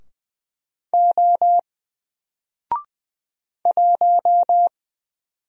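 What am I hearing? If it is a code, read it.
Morse code, heard as a keyed tone, about 20 wpm, 700 Hz. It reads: O 1
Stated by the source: Single letter and number Morse drill at 20 wpm, from N4LCD.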